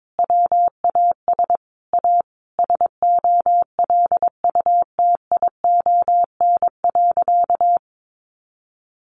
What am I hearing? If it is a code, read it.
Morse code, heard as a keyed tone, about 22 wpm, 700 Hz. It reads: WAS A SOLUTION.
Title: Sentences from Top 1300 Words 22wpm